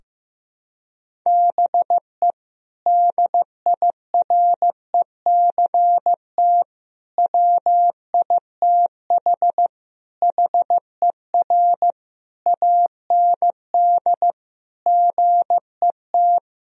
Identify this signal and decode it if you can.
Morse code, heard as a keyed tone, about 15 wpm, 700 Hz. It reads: BE DIRECT WITH HER AND GET
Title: Sentences from Top 1000 Words - Review Part 2 - 15wpm